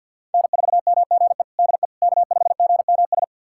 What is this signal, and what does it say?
Morse code, heard as a keyed tone, about 50 wpm, 700 Hz. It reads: N4MGE DE K5GMS
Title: USA Callsign DE Callsign - 5 Characters - 50wpm